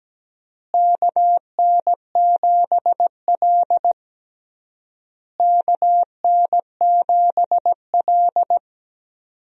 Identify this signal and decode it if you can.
Morse code, heard as a keyed tone, about 17 wpm, 700 Hz. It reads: KN7L KN7L
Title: USA Call Signs - 4 Characters - Repeated 3x - 17wpm